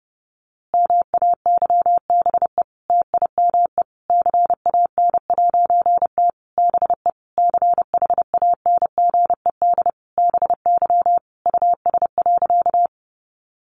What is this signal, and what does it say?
Morse code, heard as a keyed tone, about 30 wpm, 700 Hz. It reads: MAYBE TIME CAN'T BE CHANGED BY US.
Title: A Sound Of Thunder 30wpm